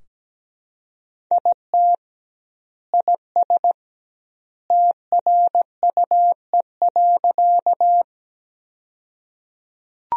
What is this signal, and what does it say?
Morse code, heard as a keyed tone, about 17 wpm, 700 Hz. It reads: IT IS TRUE.
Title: 2x Word-Spacing - Sentences from Top 400 Words - 17wpm